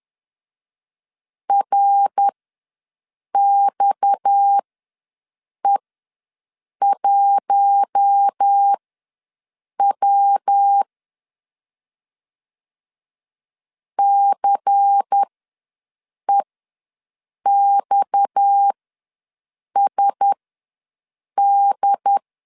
Morse code, heard as a keyed tone, about 10 wpm, spaced as for 6 wpm, 800 Hz. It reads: RXE1W CEXSD
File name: LEZIONI / 6 PKX0 / 6 PKX ws 10wpm 3706